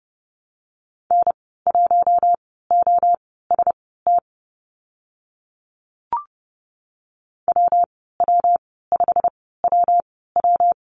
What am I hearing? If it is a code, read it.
Morse code, heard as a keyed tone, about 30 wpm, spaced as for 17 wpm, 700 Hz. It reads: N1OST WW5WW
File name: Lesson 19g - Letter w - Call Signs 30wpm